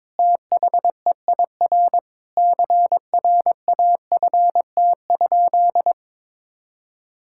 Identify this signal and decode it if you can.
Morse code, heard as a keyed tone, about 22 wpm, 700 Hz. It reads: THEIR CRAFT?
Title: Sentences from Top 1000 Words - Review Part 6 - 22wpm